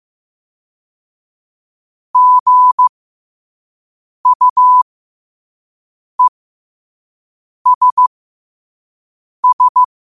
Morse code, heard as a keyed tone, about 15 wpm, spaced as for 5 wpm, 1000 Hz. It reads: GUESS
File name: qotdfiles / 5wpm